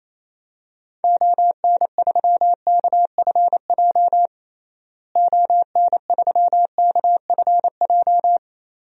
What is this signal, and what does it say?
Morse code, heard as a keyed tone, about 28 wpm, 700 Hz. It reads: ON3KFJ ON3KFJ
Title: DXCC-100 Call Signs - 6 Characters - Repeated 3x - 28wpm